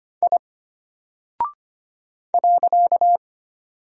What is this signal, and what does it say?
Morse code, heard as a keyed tone, about 25 wpm, 700 Hz. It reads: I .
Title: Single Character (Letter:Number:Punctuation:Prosign) Alt 25wpm